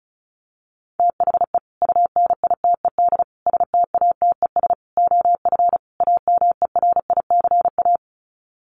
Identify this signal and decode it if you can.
Morse code, heard as a keyed tone, about 34 wpm, 700 Hz. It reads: THE UNITED STATES OF AMERICA
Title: Sets of 5 Words 35wpm